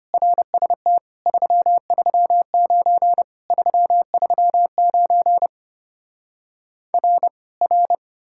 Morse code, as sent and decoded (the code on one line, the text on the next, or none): .-. ... - / ...-- ...-- ----. / ...-- ...-- ----. / .-. / .-.
RST 339 339 R R